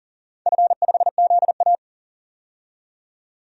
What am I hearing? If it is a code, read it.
Morse code, heard as a keyed tone, about 38 wpm, 700 Hz. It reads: F5ZA
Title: DXCC-100 Call Signs - 4 Characters - 40wpm